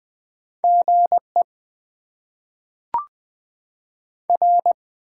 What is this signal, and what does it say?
Morse code, heard as a keyed tone, about 20 wpm, 700 Hz. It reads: GE R
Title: Top 20 QSO Elements - 8x Character Spacing - 20wpm